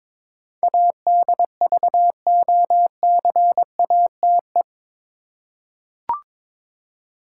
Morse code, heard as a keyed tone, about 22 wpm, 700 Hz. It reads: ADVOCATE
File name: practice / 22wpm / Eight Letter Words - 22wpm